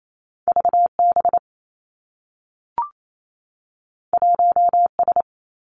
Morse code, heard as a keyed tone, about 28 wpm, 700 Hz. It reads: V B 1 S
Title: Two Challenging Letter-Number Combinations 28wpm